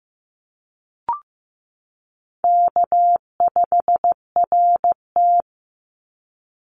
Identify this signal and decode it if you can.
Morse code, heard as a keyed tone, about 15 wpm, 700 Hz. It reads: K5RT
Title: Easy Call Signs - 15wpm